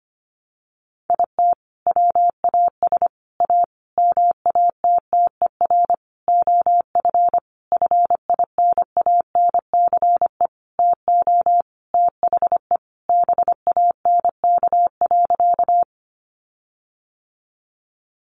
Morse code, heard as a keyed tone, about 25 wpm, 700 Hz. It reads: IT WAS A MATTER OF FINANCE TO THE BANK.
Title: Sentences from Top 1500 Words 25wpm